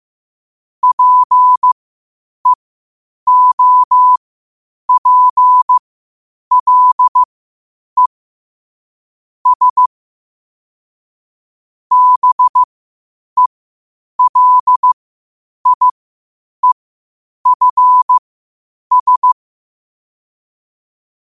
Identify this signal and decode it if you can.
Morse code, heard as a keyed tone, about 15 wpm, spaced as for 8 wpm, 1000 Hz. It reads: PEOPLE S BELIEFS
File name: qotdfiles / 8wpm